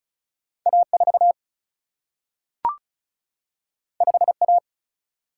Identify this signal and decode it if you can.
Morse code, heard as a keyed tone, about 36 wpm, 700 Hz. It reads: A4 5A
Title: Single-Letter Single-Number 35wpm